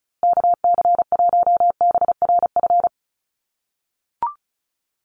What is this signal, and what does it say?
Morse code, heard as a keyed tone, about 34 wpm, 700 Hz. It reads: KC1BRF
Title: USA Call Signs Repeated 2x 35wpm